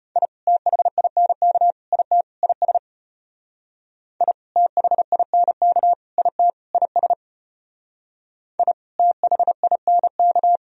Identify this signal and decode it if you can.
Morse code, heard as a keyed tone, about 31 wpm, 700 Hz. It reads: I THINK IT IS I THINK IT IS I THINK
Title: Speed-Racing - Sets of 4 Words 25wpm